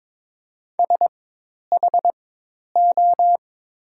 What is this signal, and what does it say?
Morse code, heard as a keyed tone, about 23 wpm, spaced as for 10 wpm, 700 Hz. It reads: SHO
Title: 4x Character Spacing - Top 500 Words - 22wpm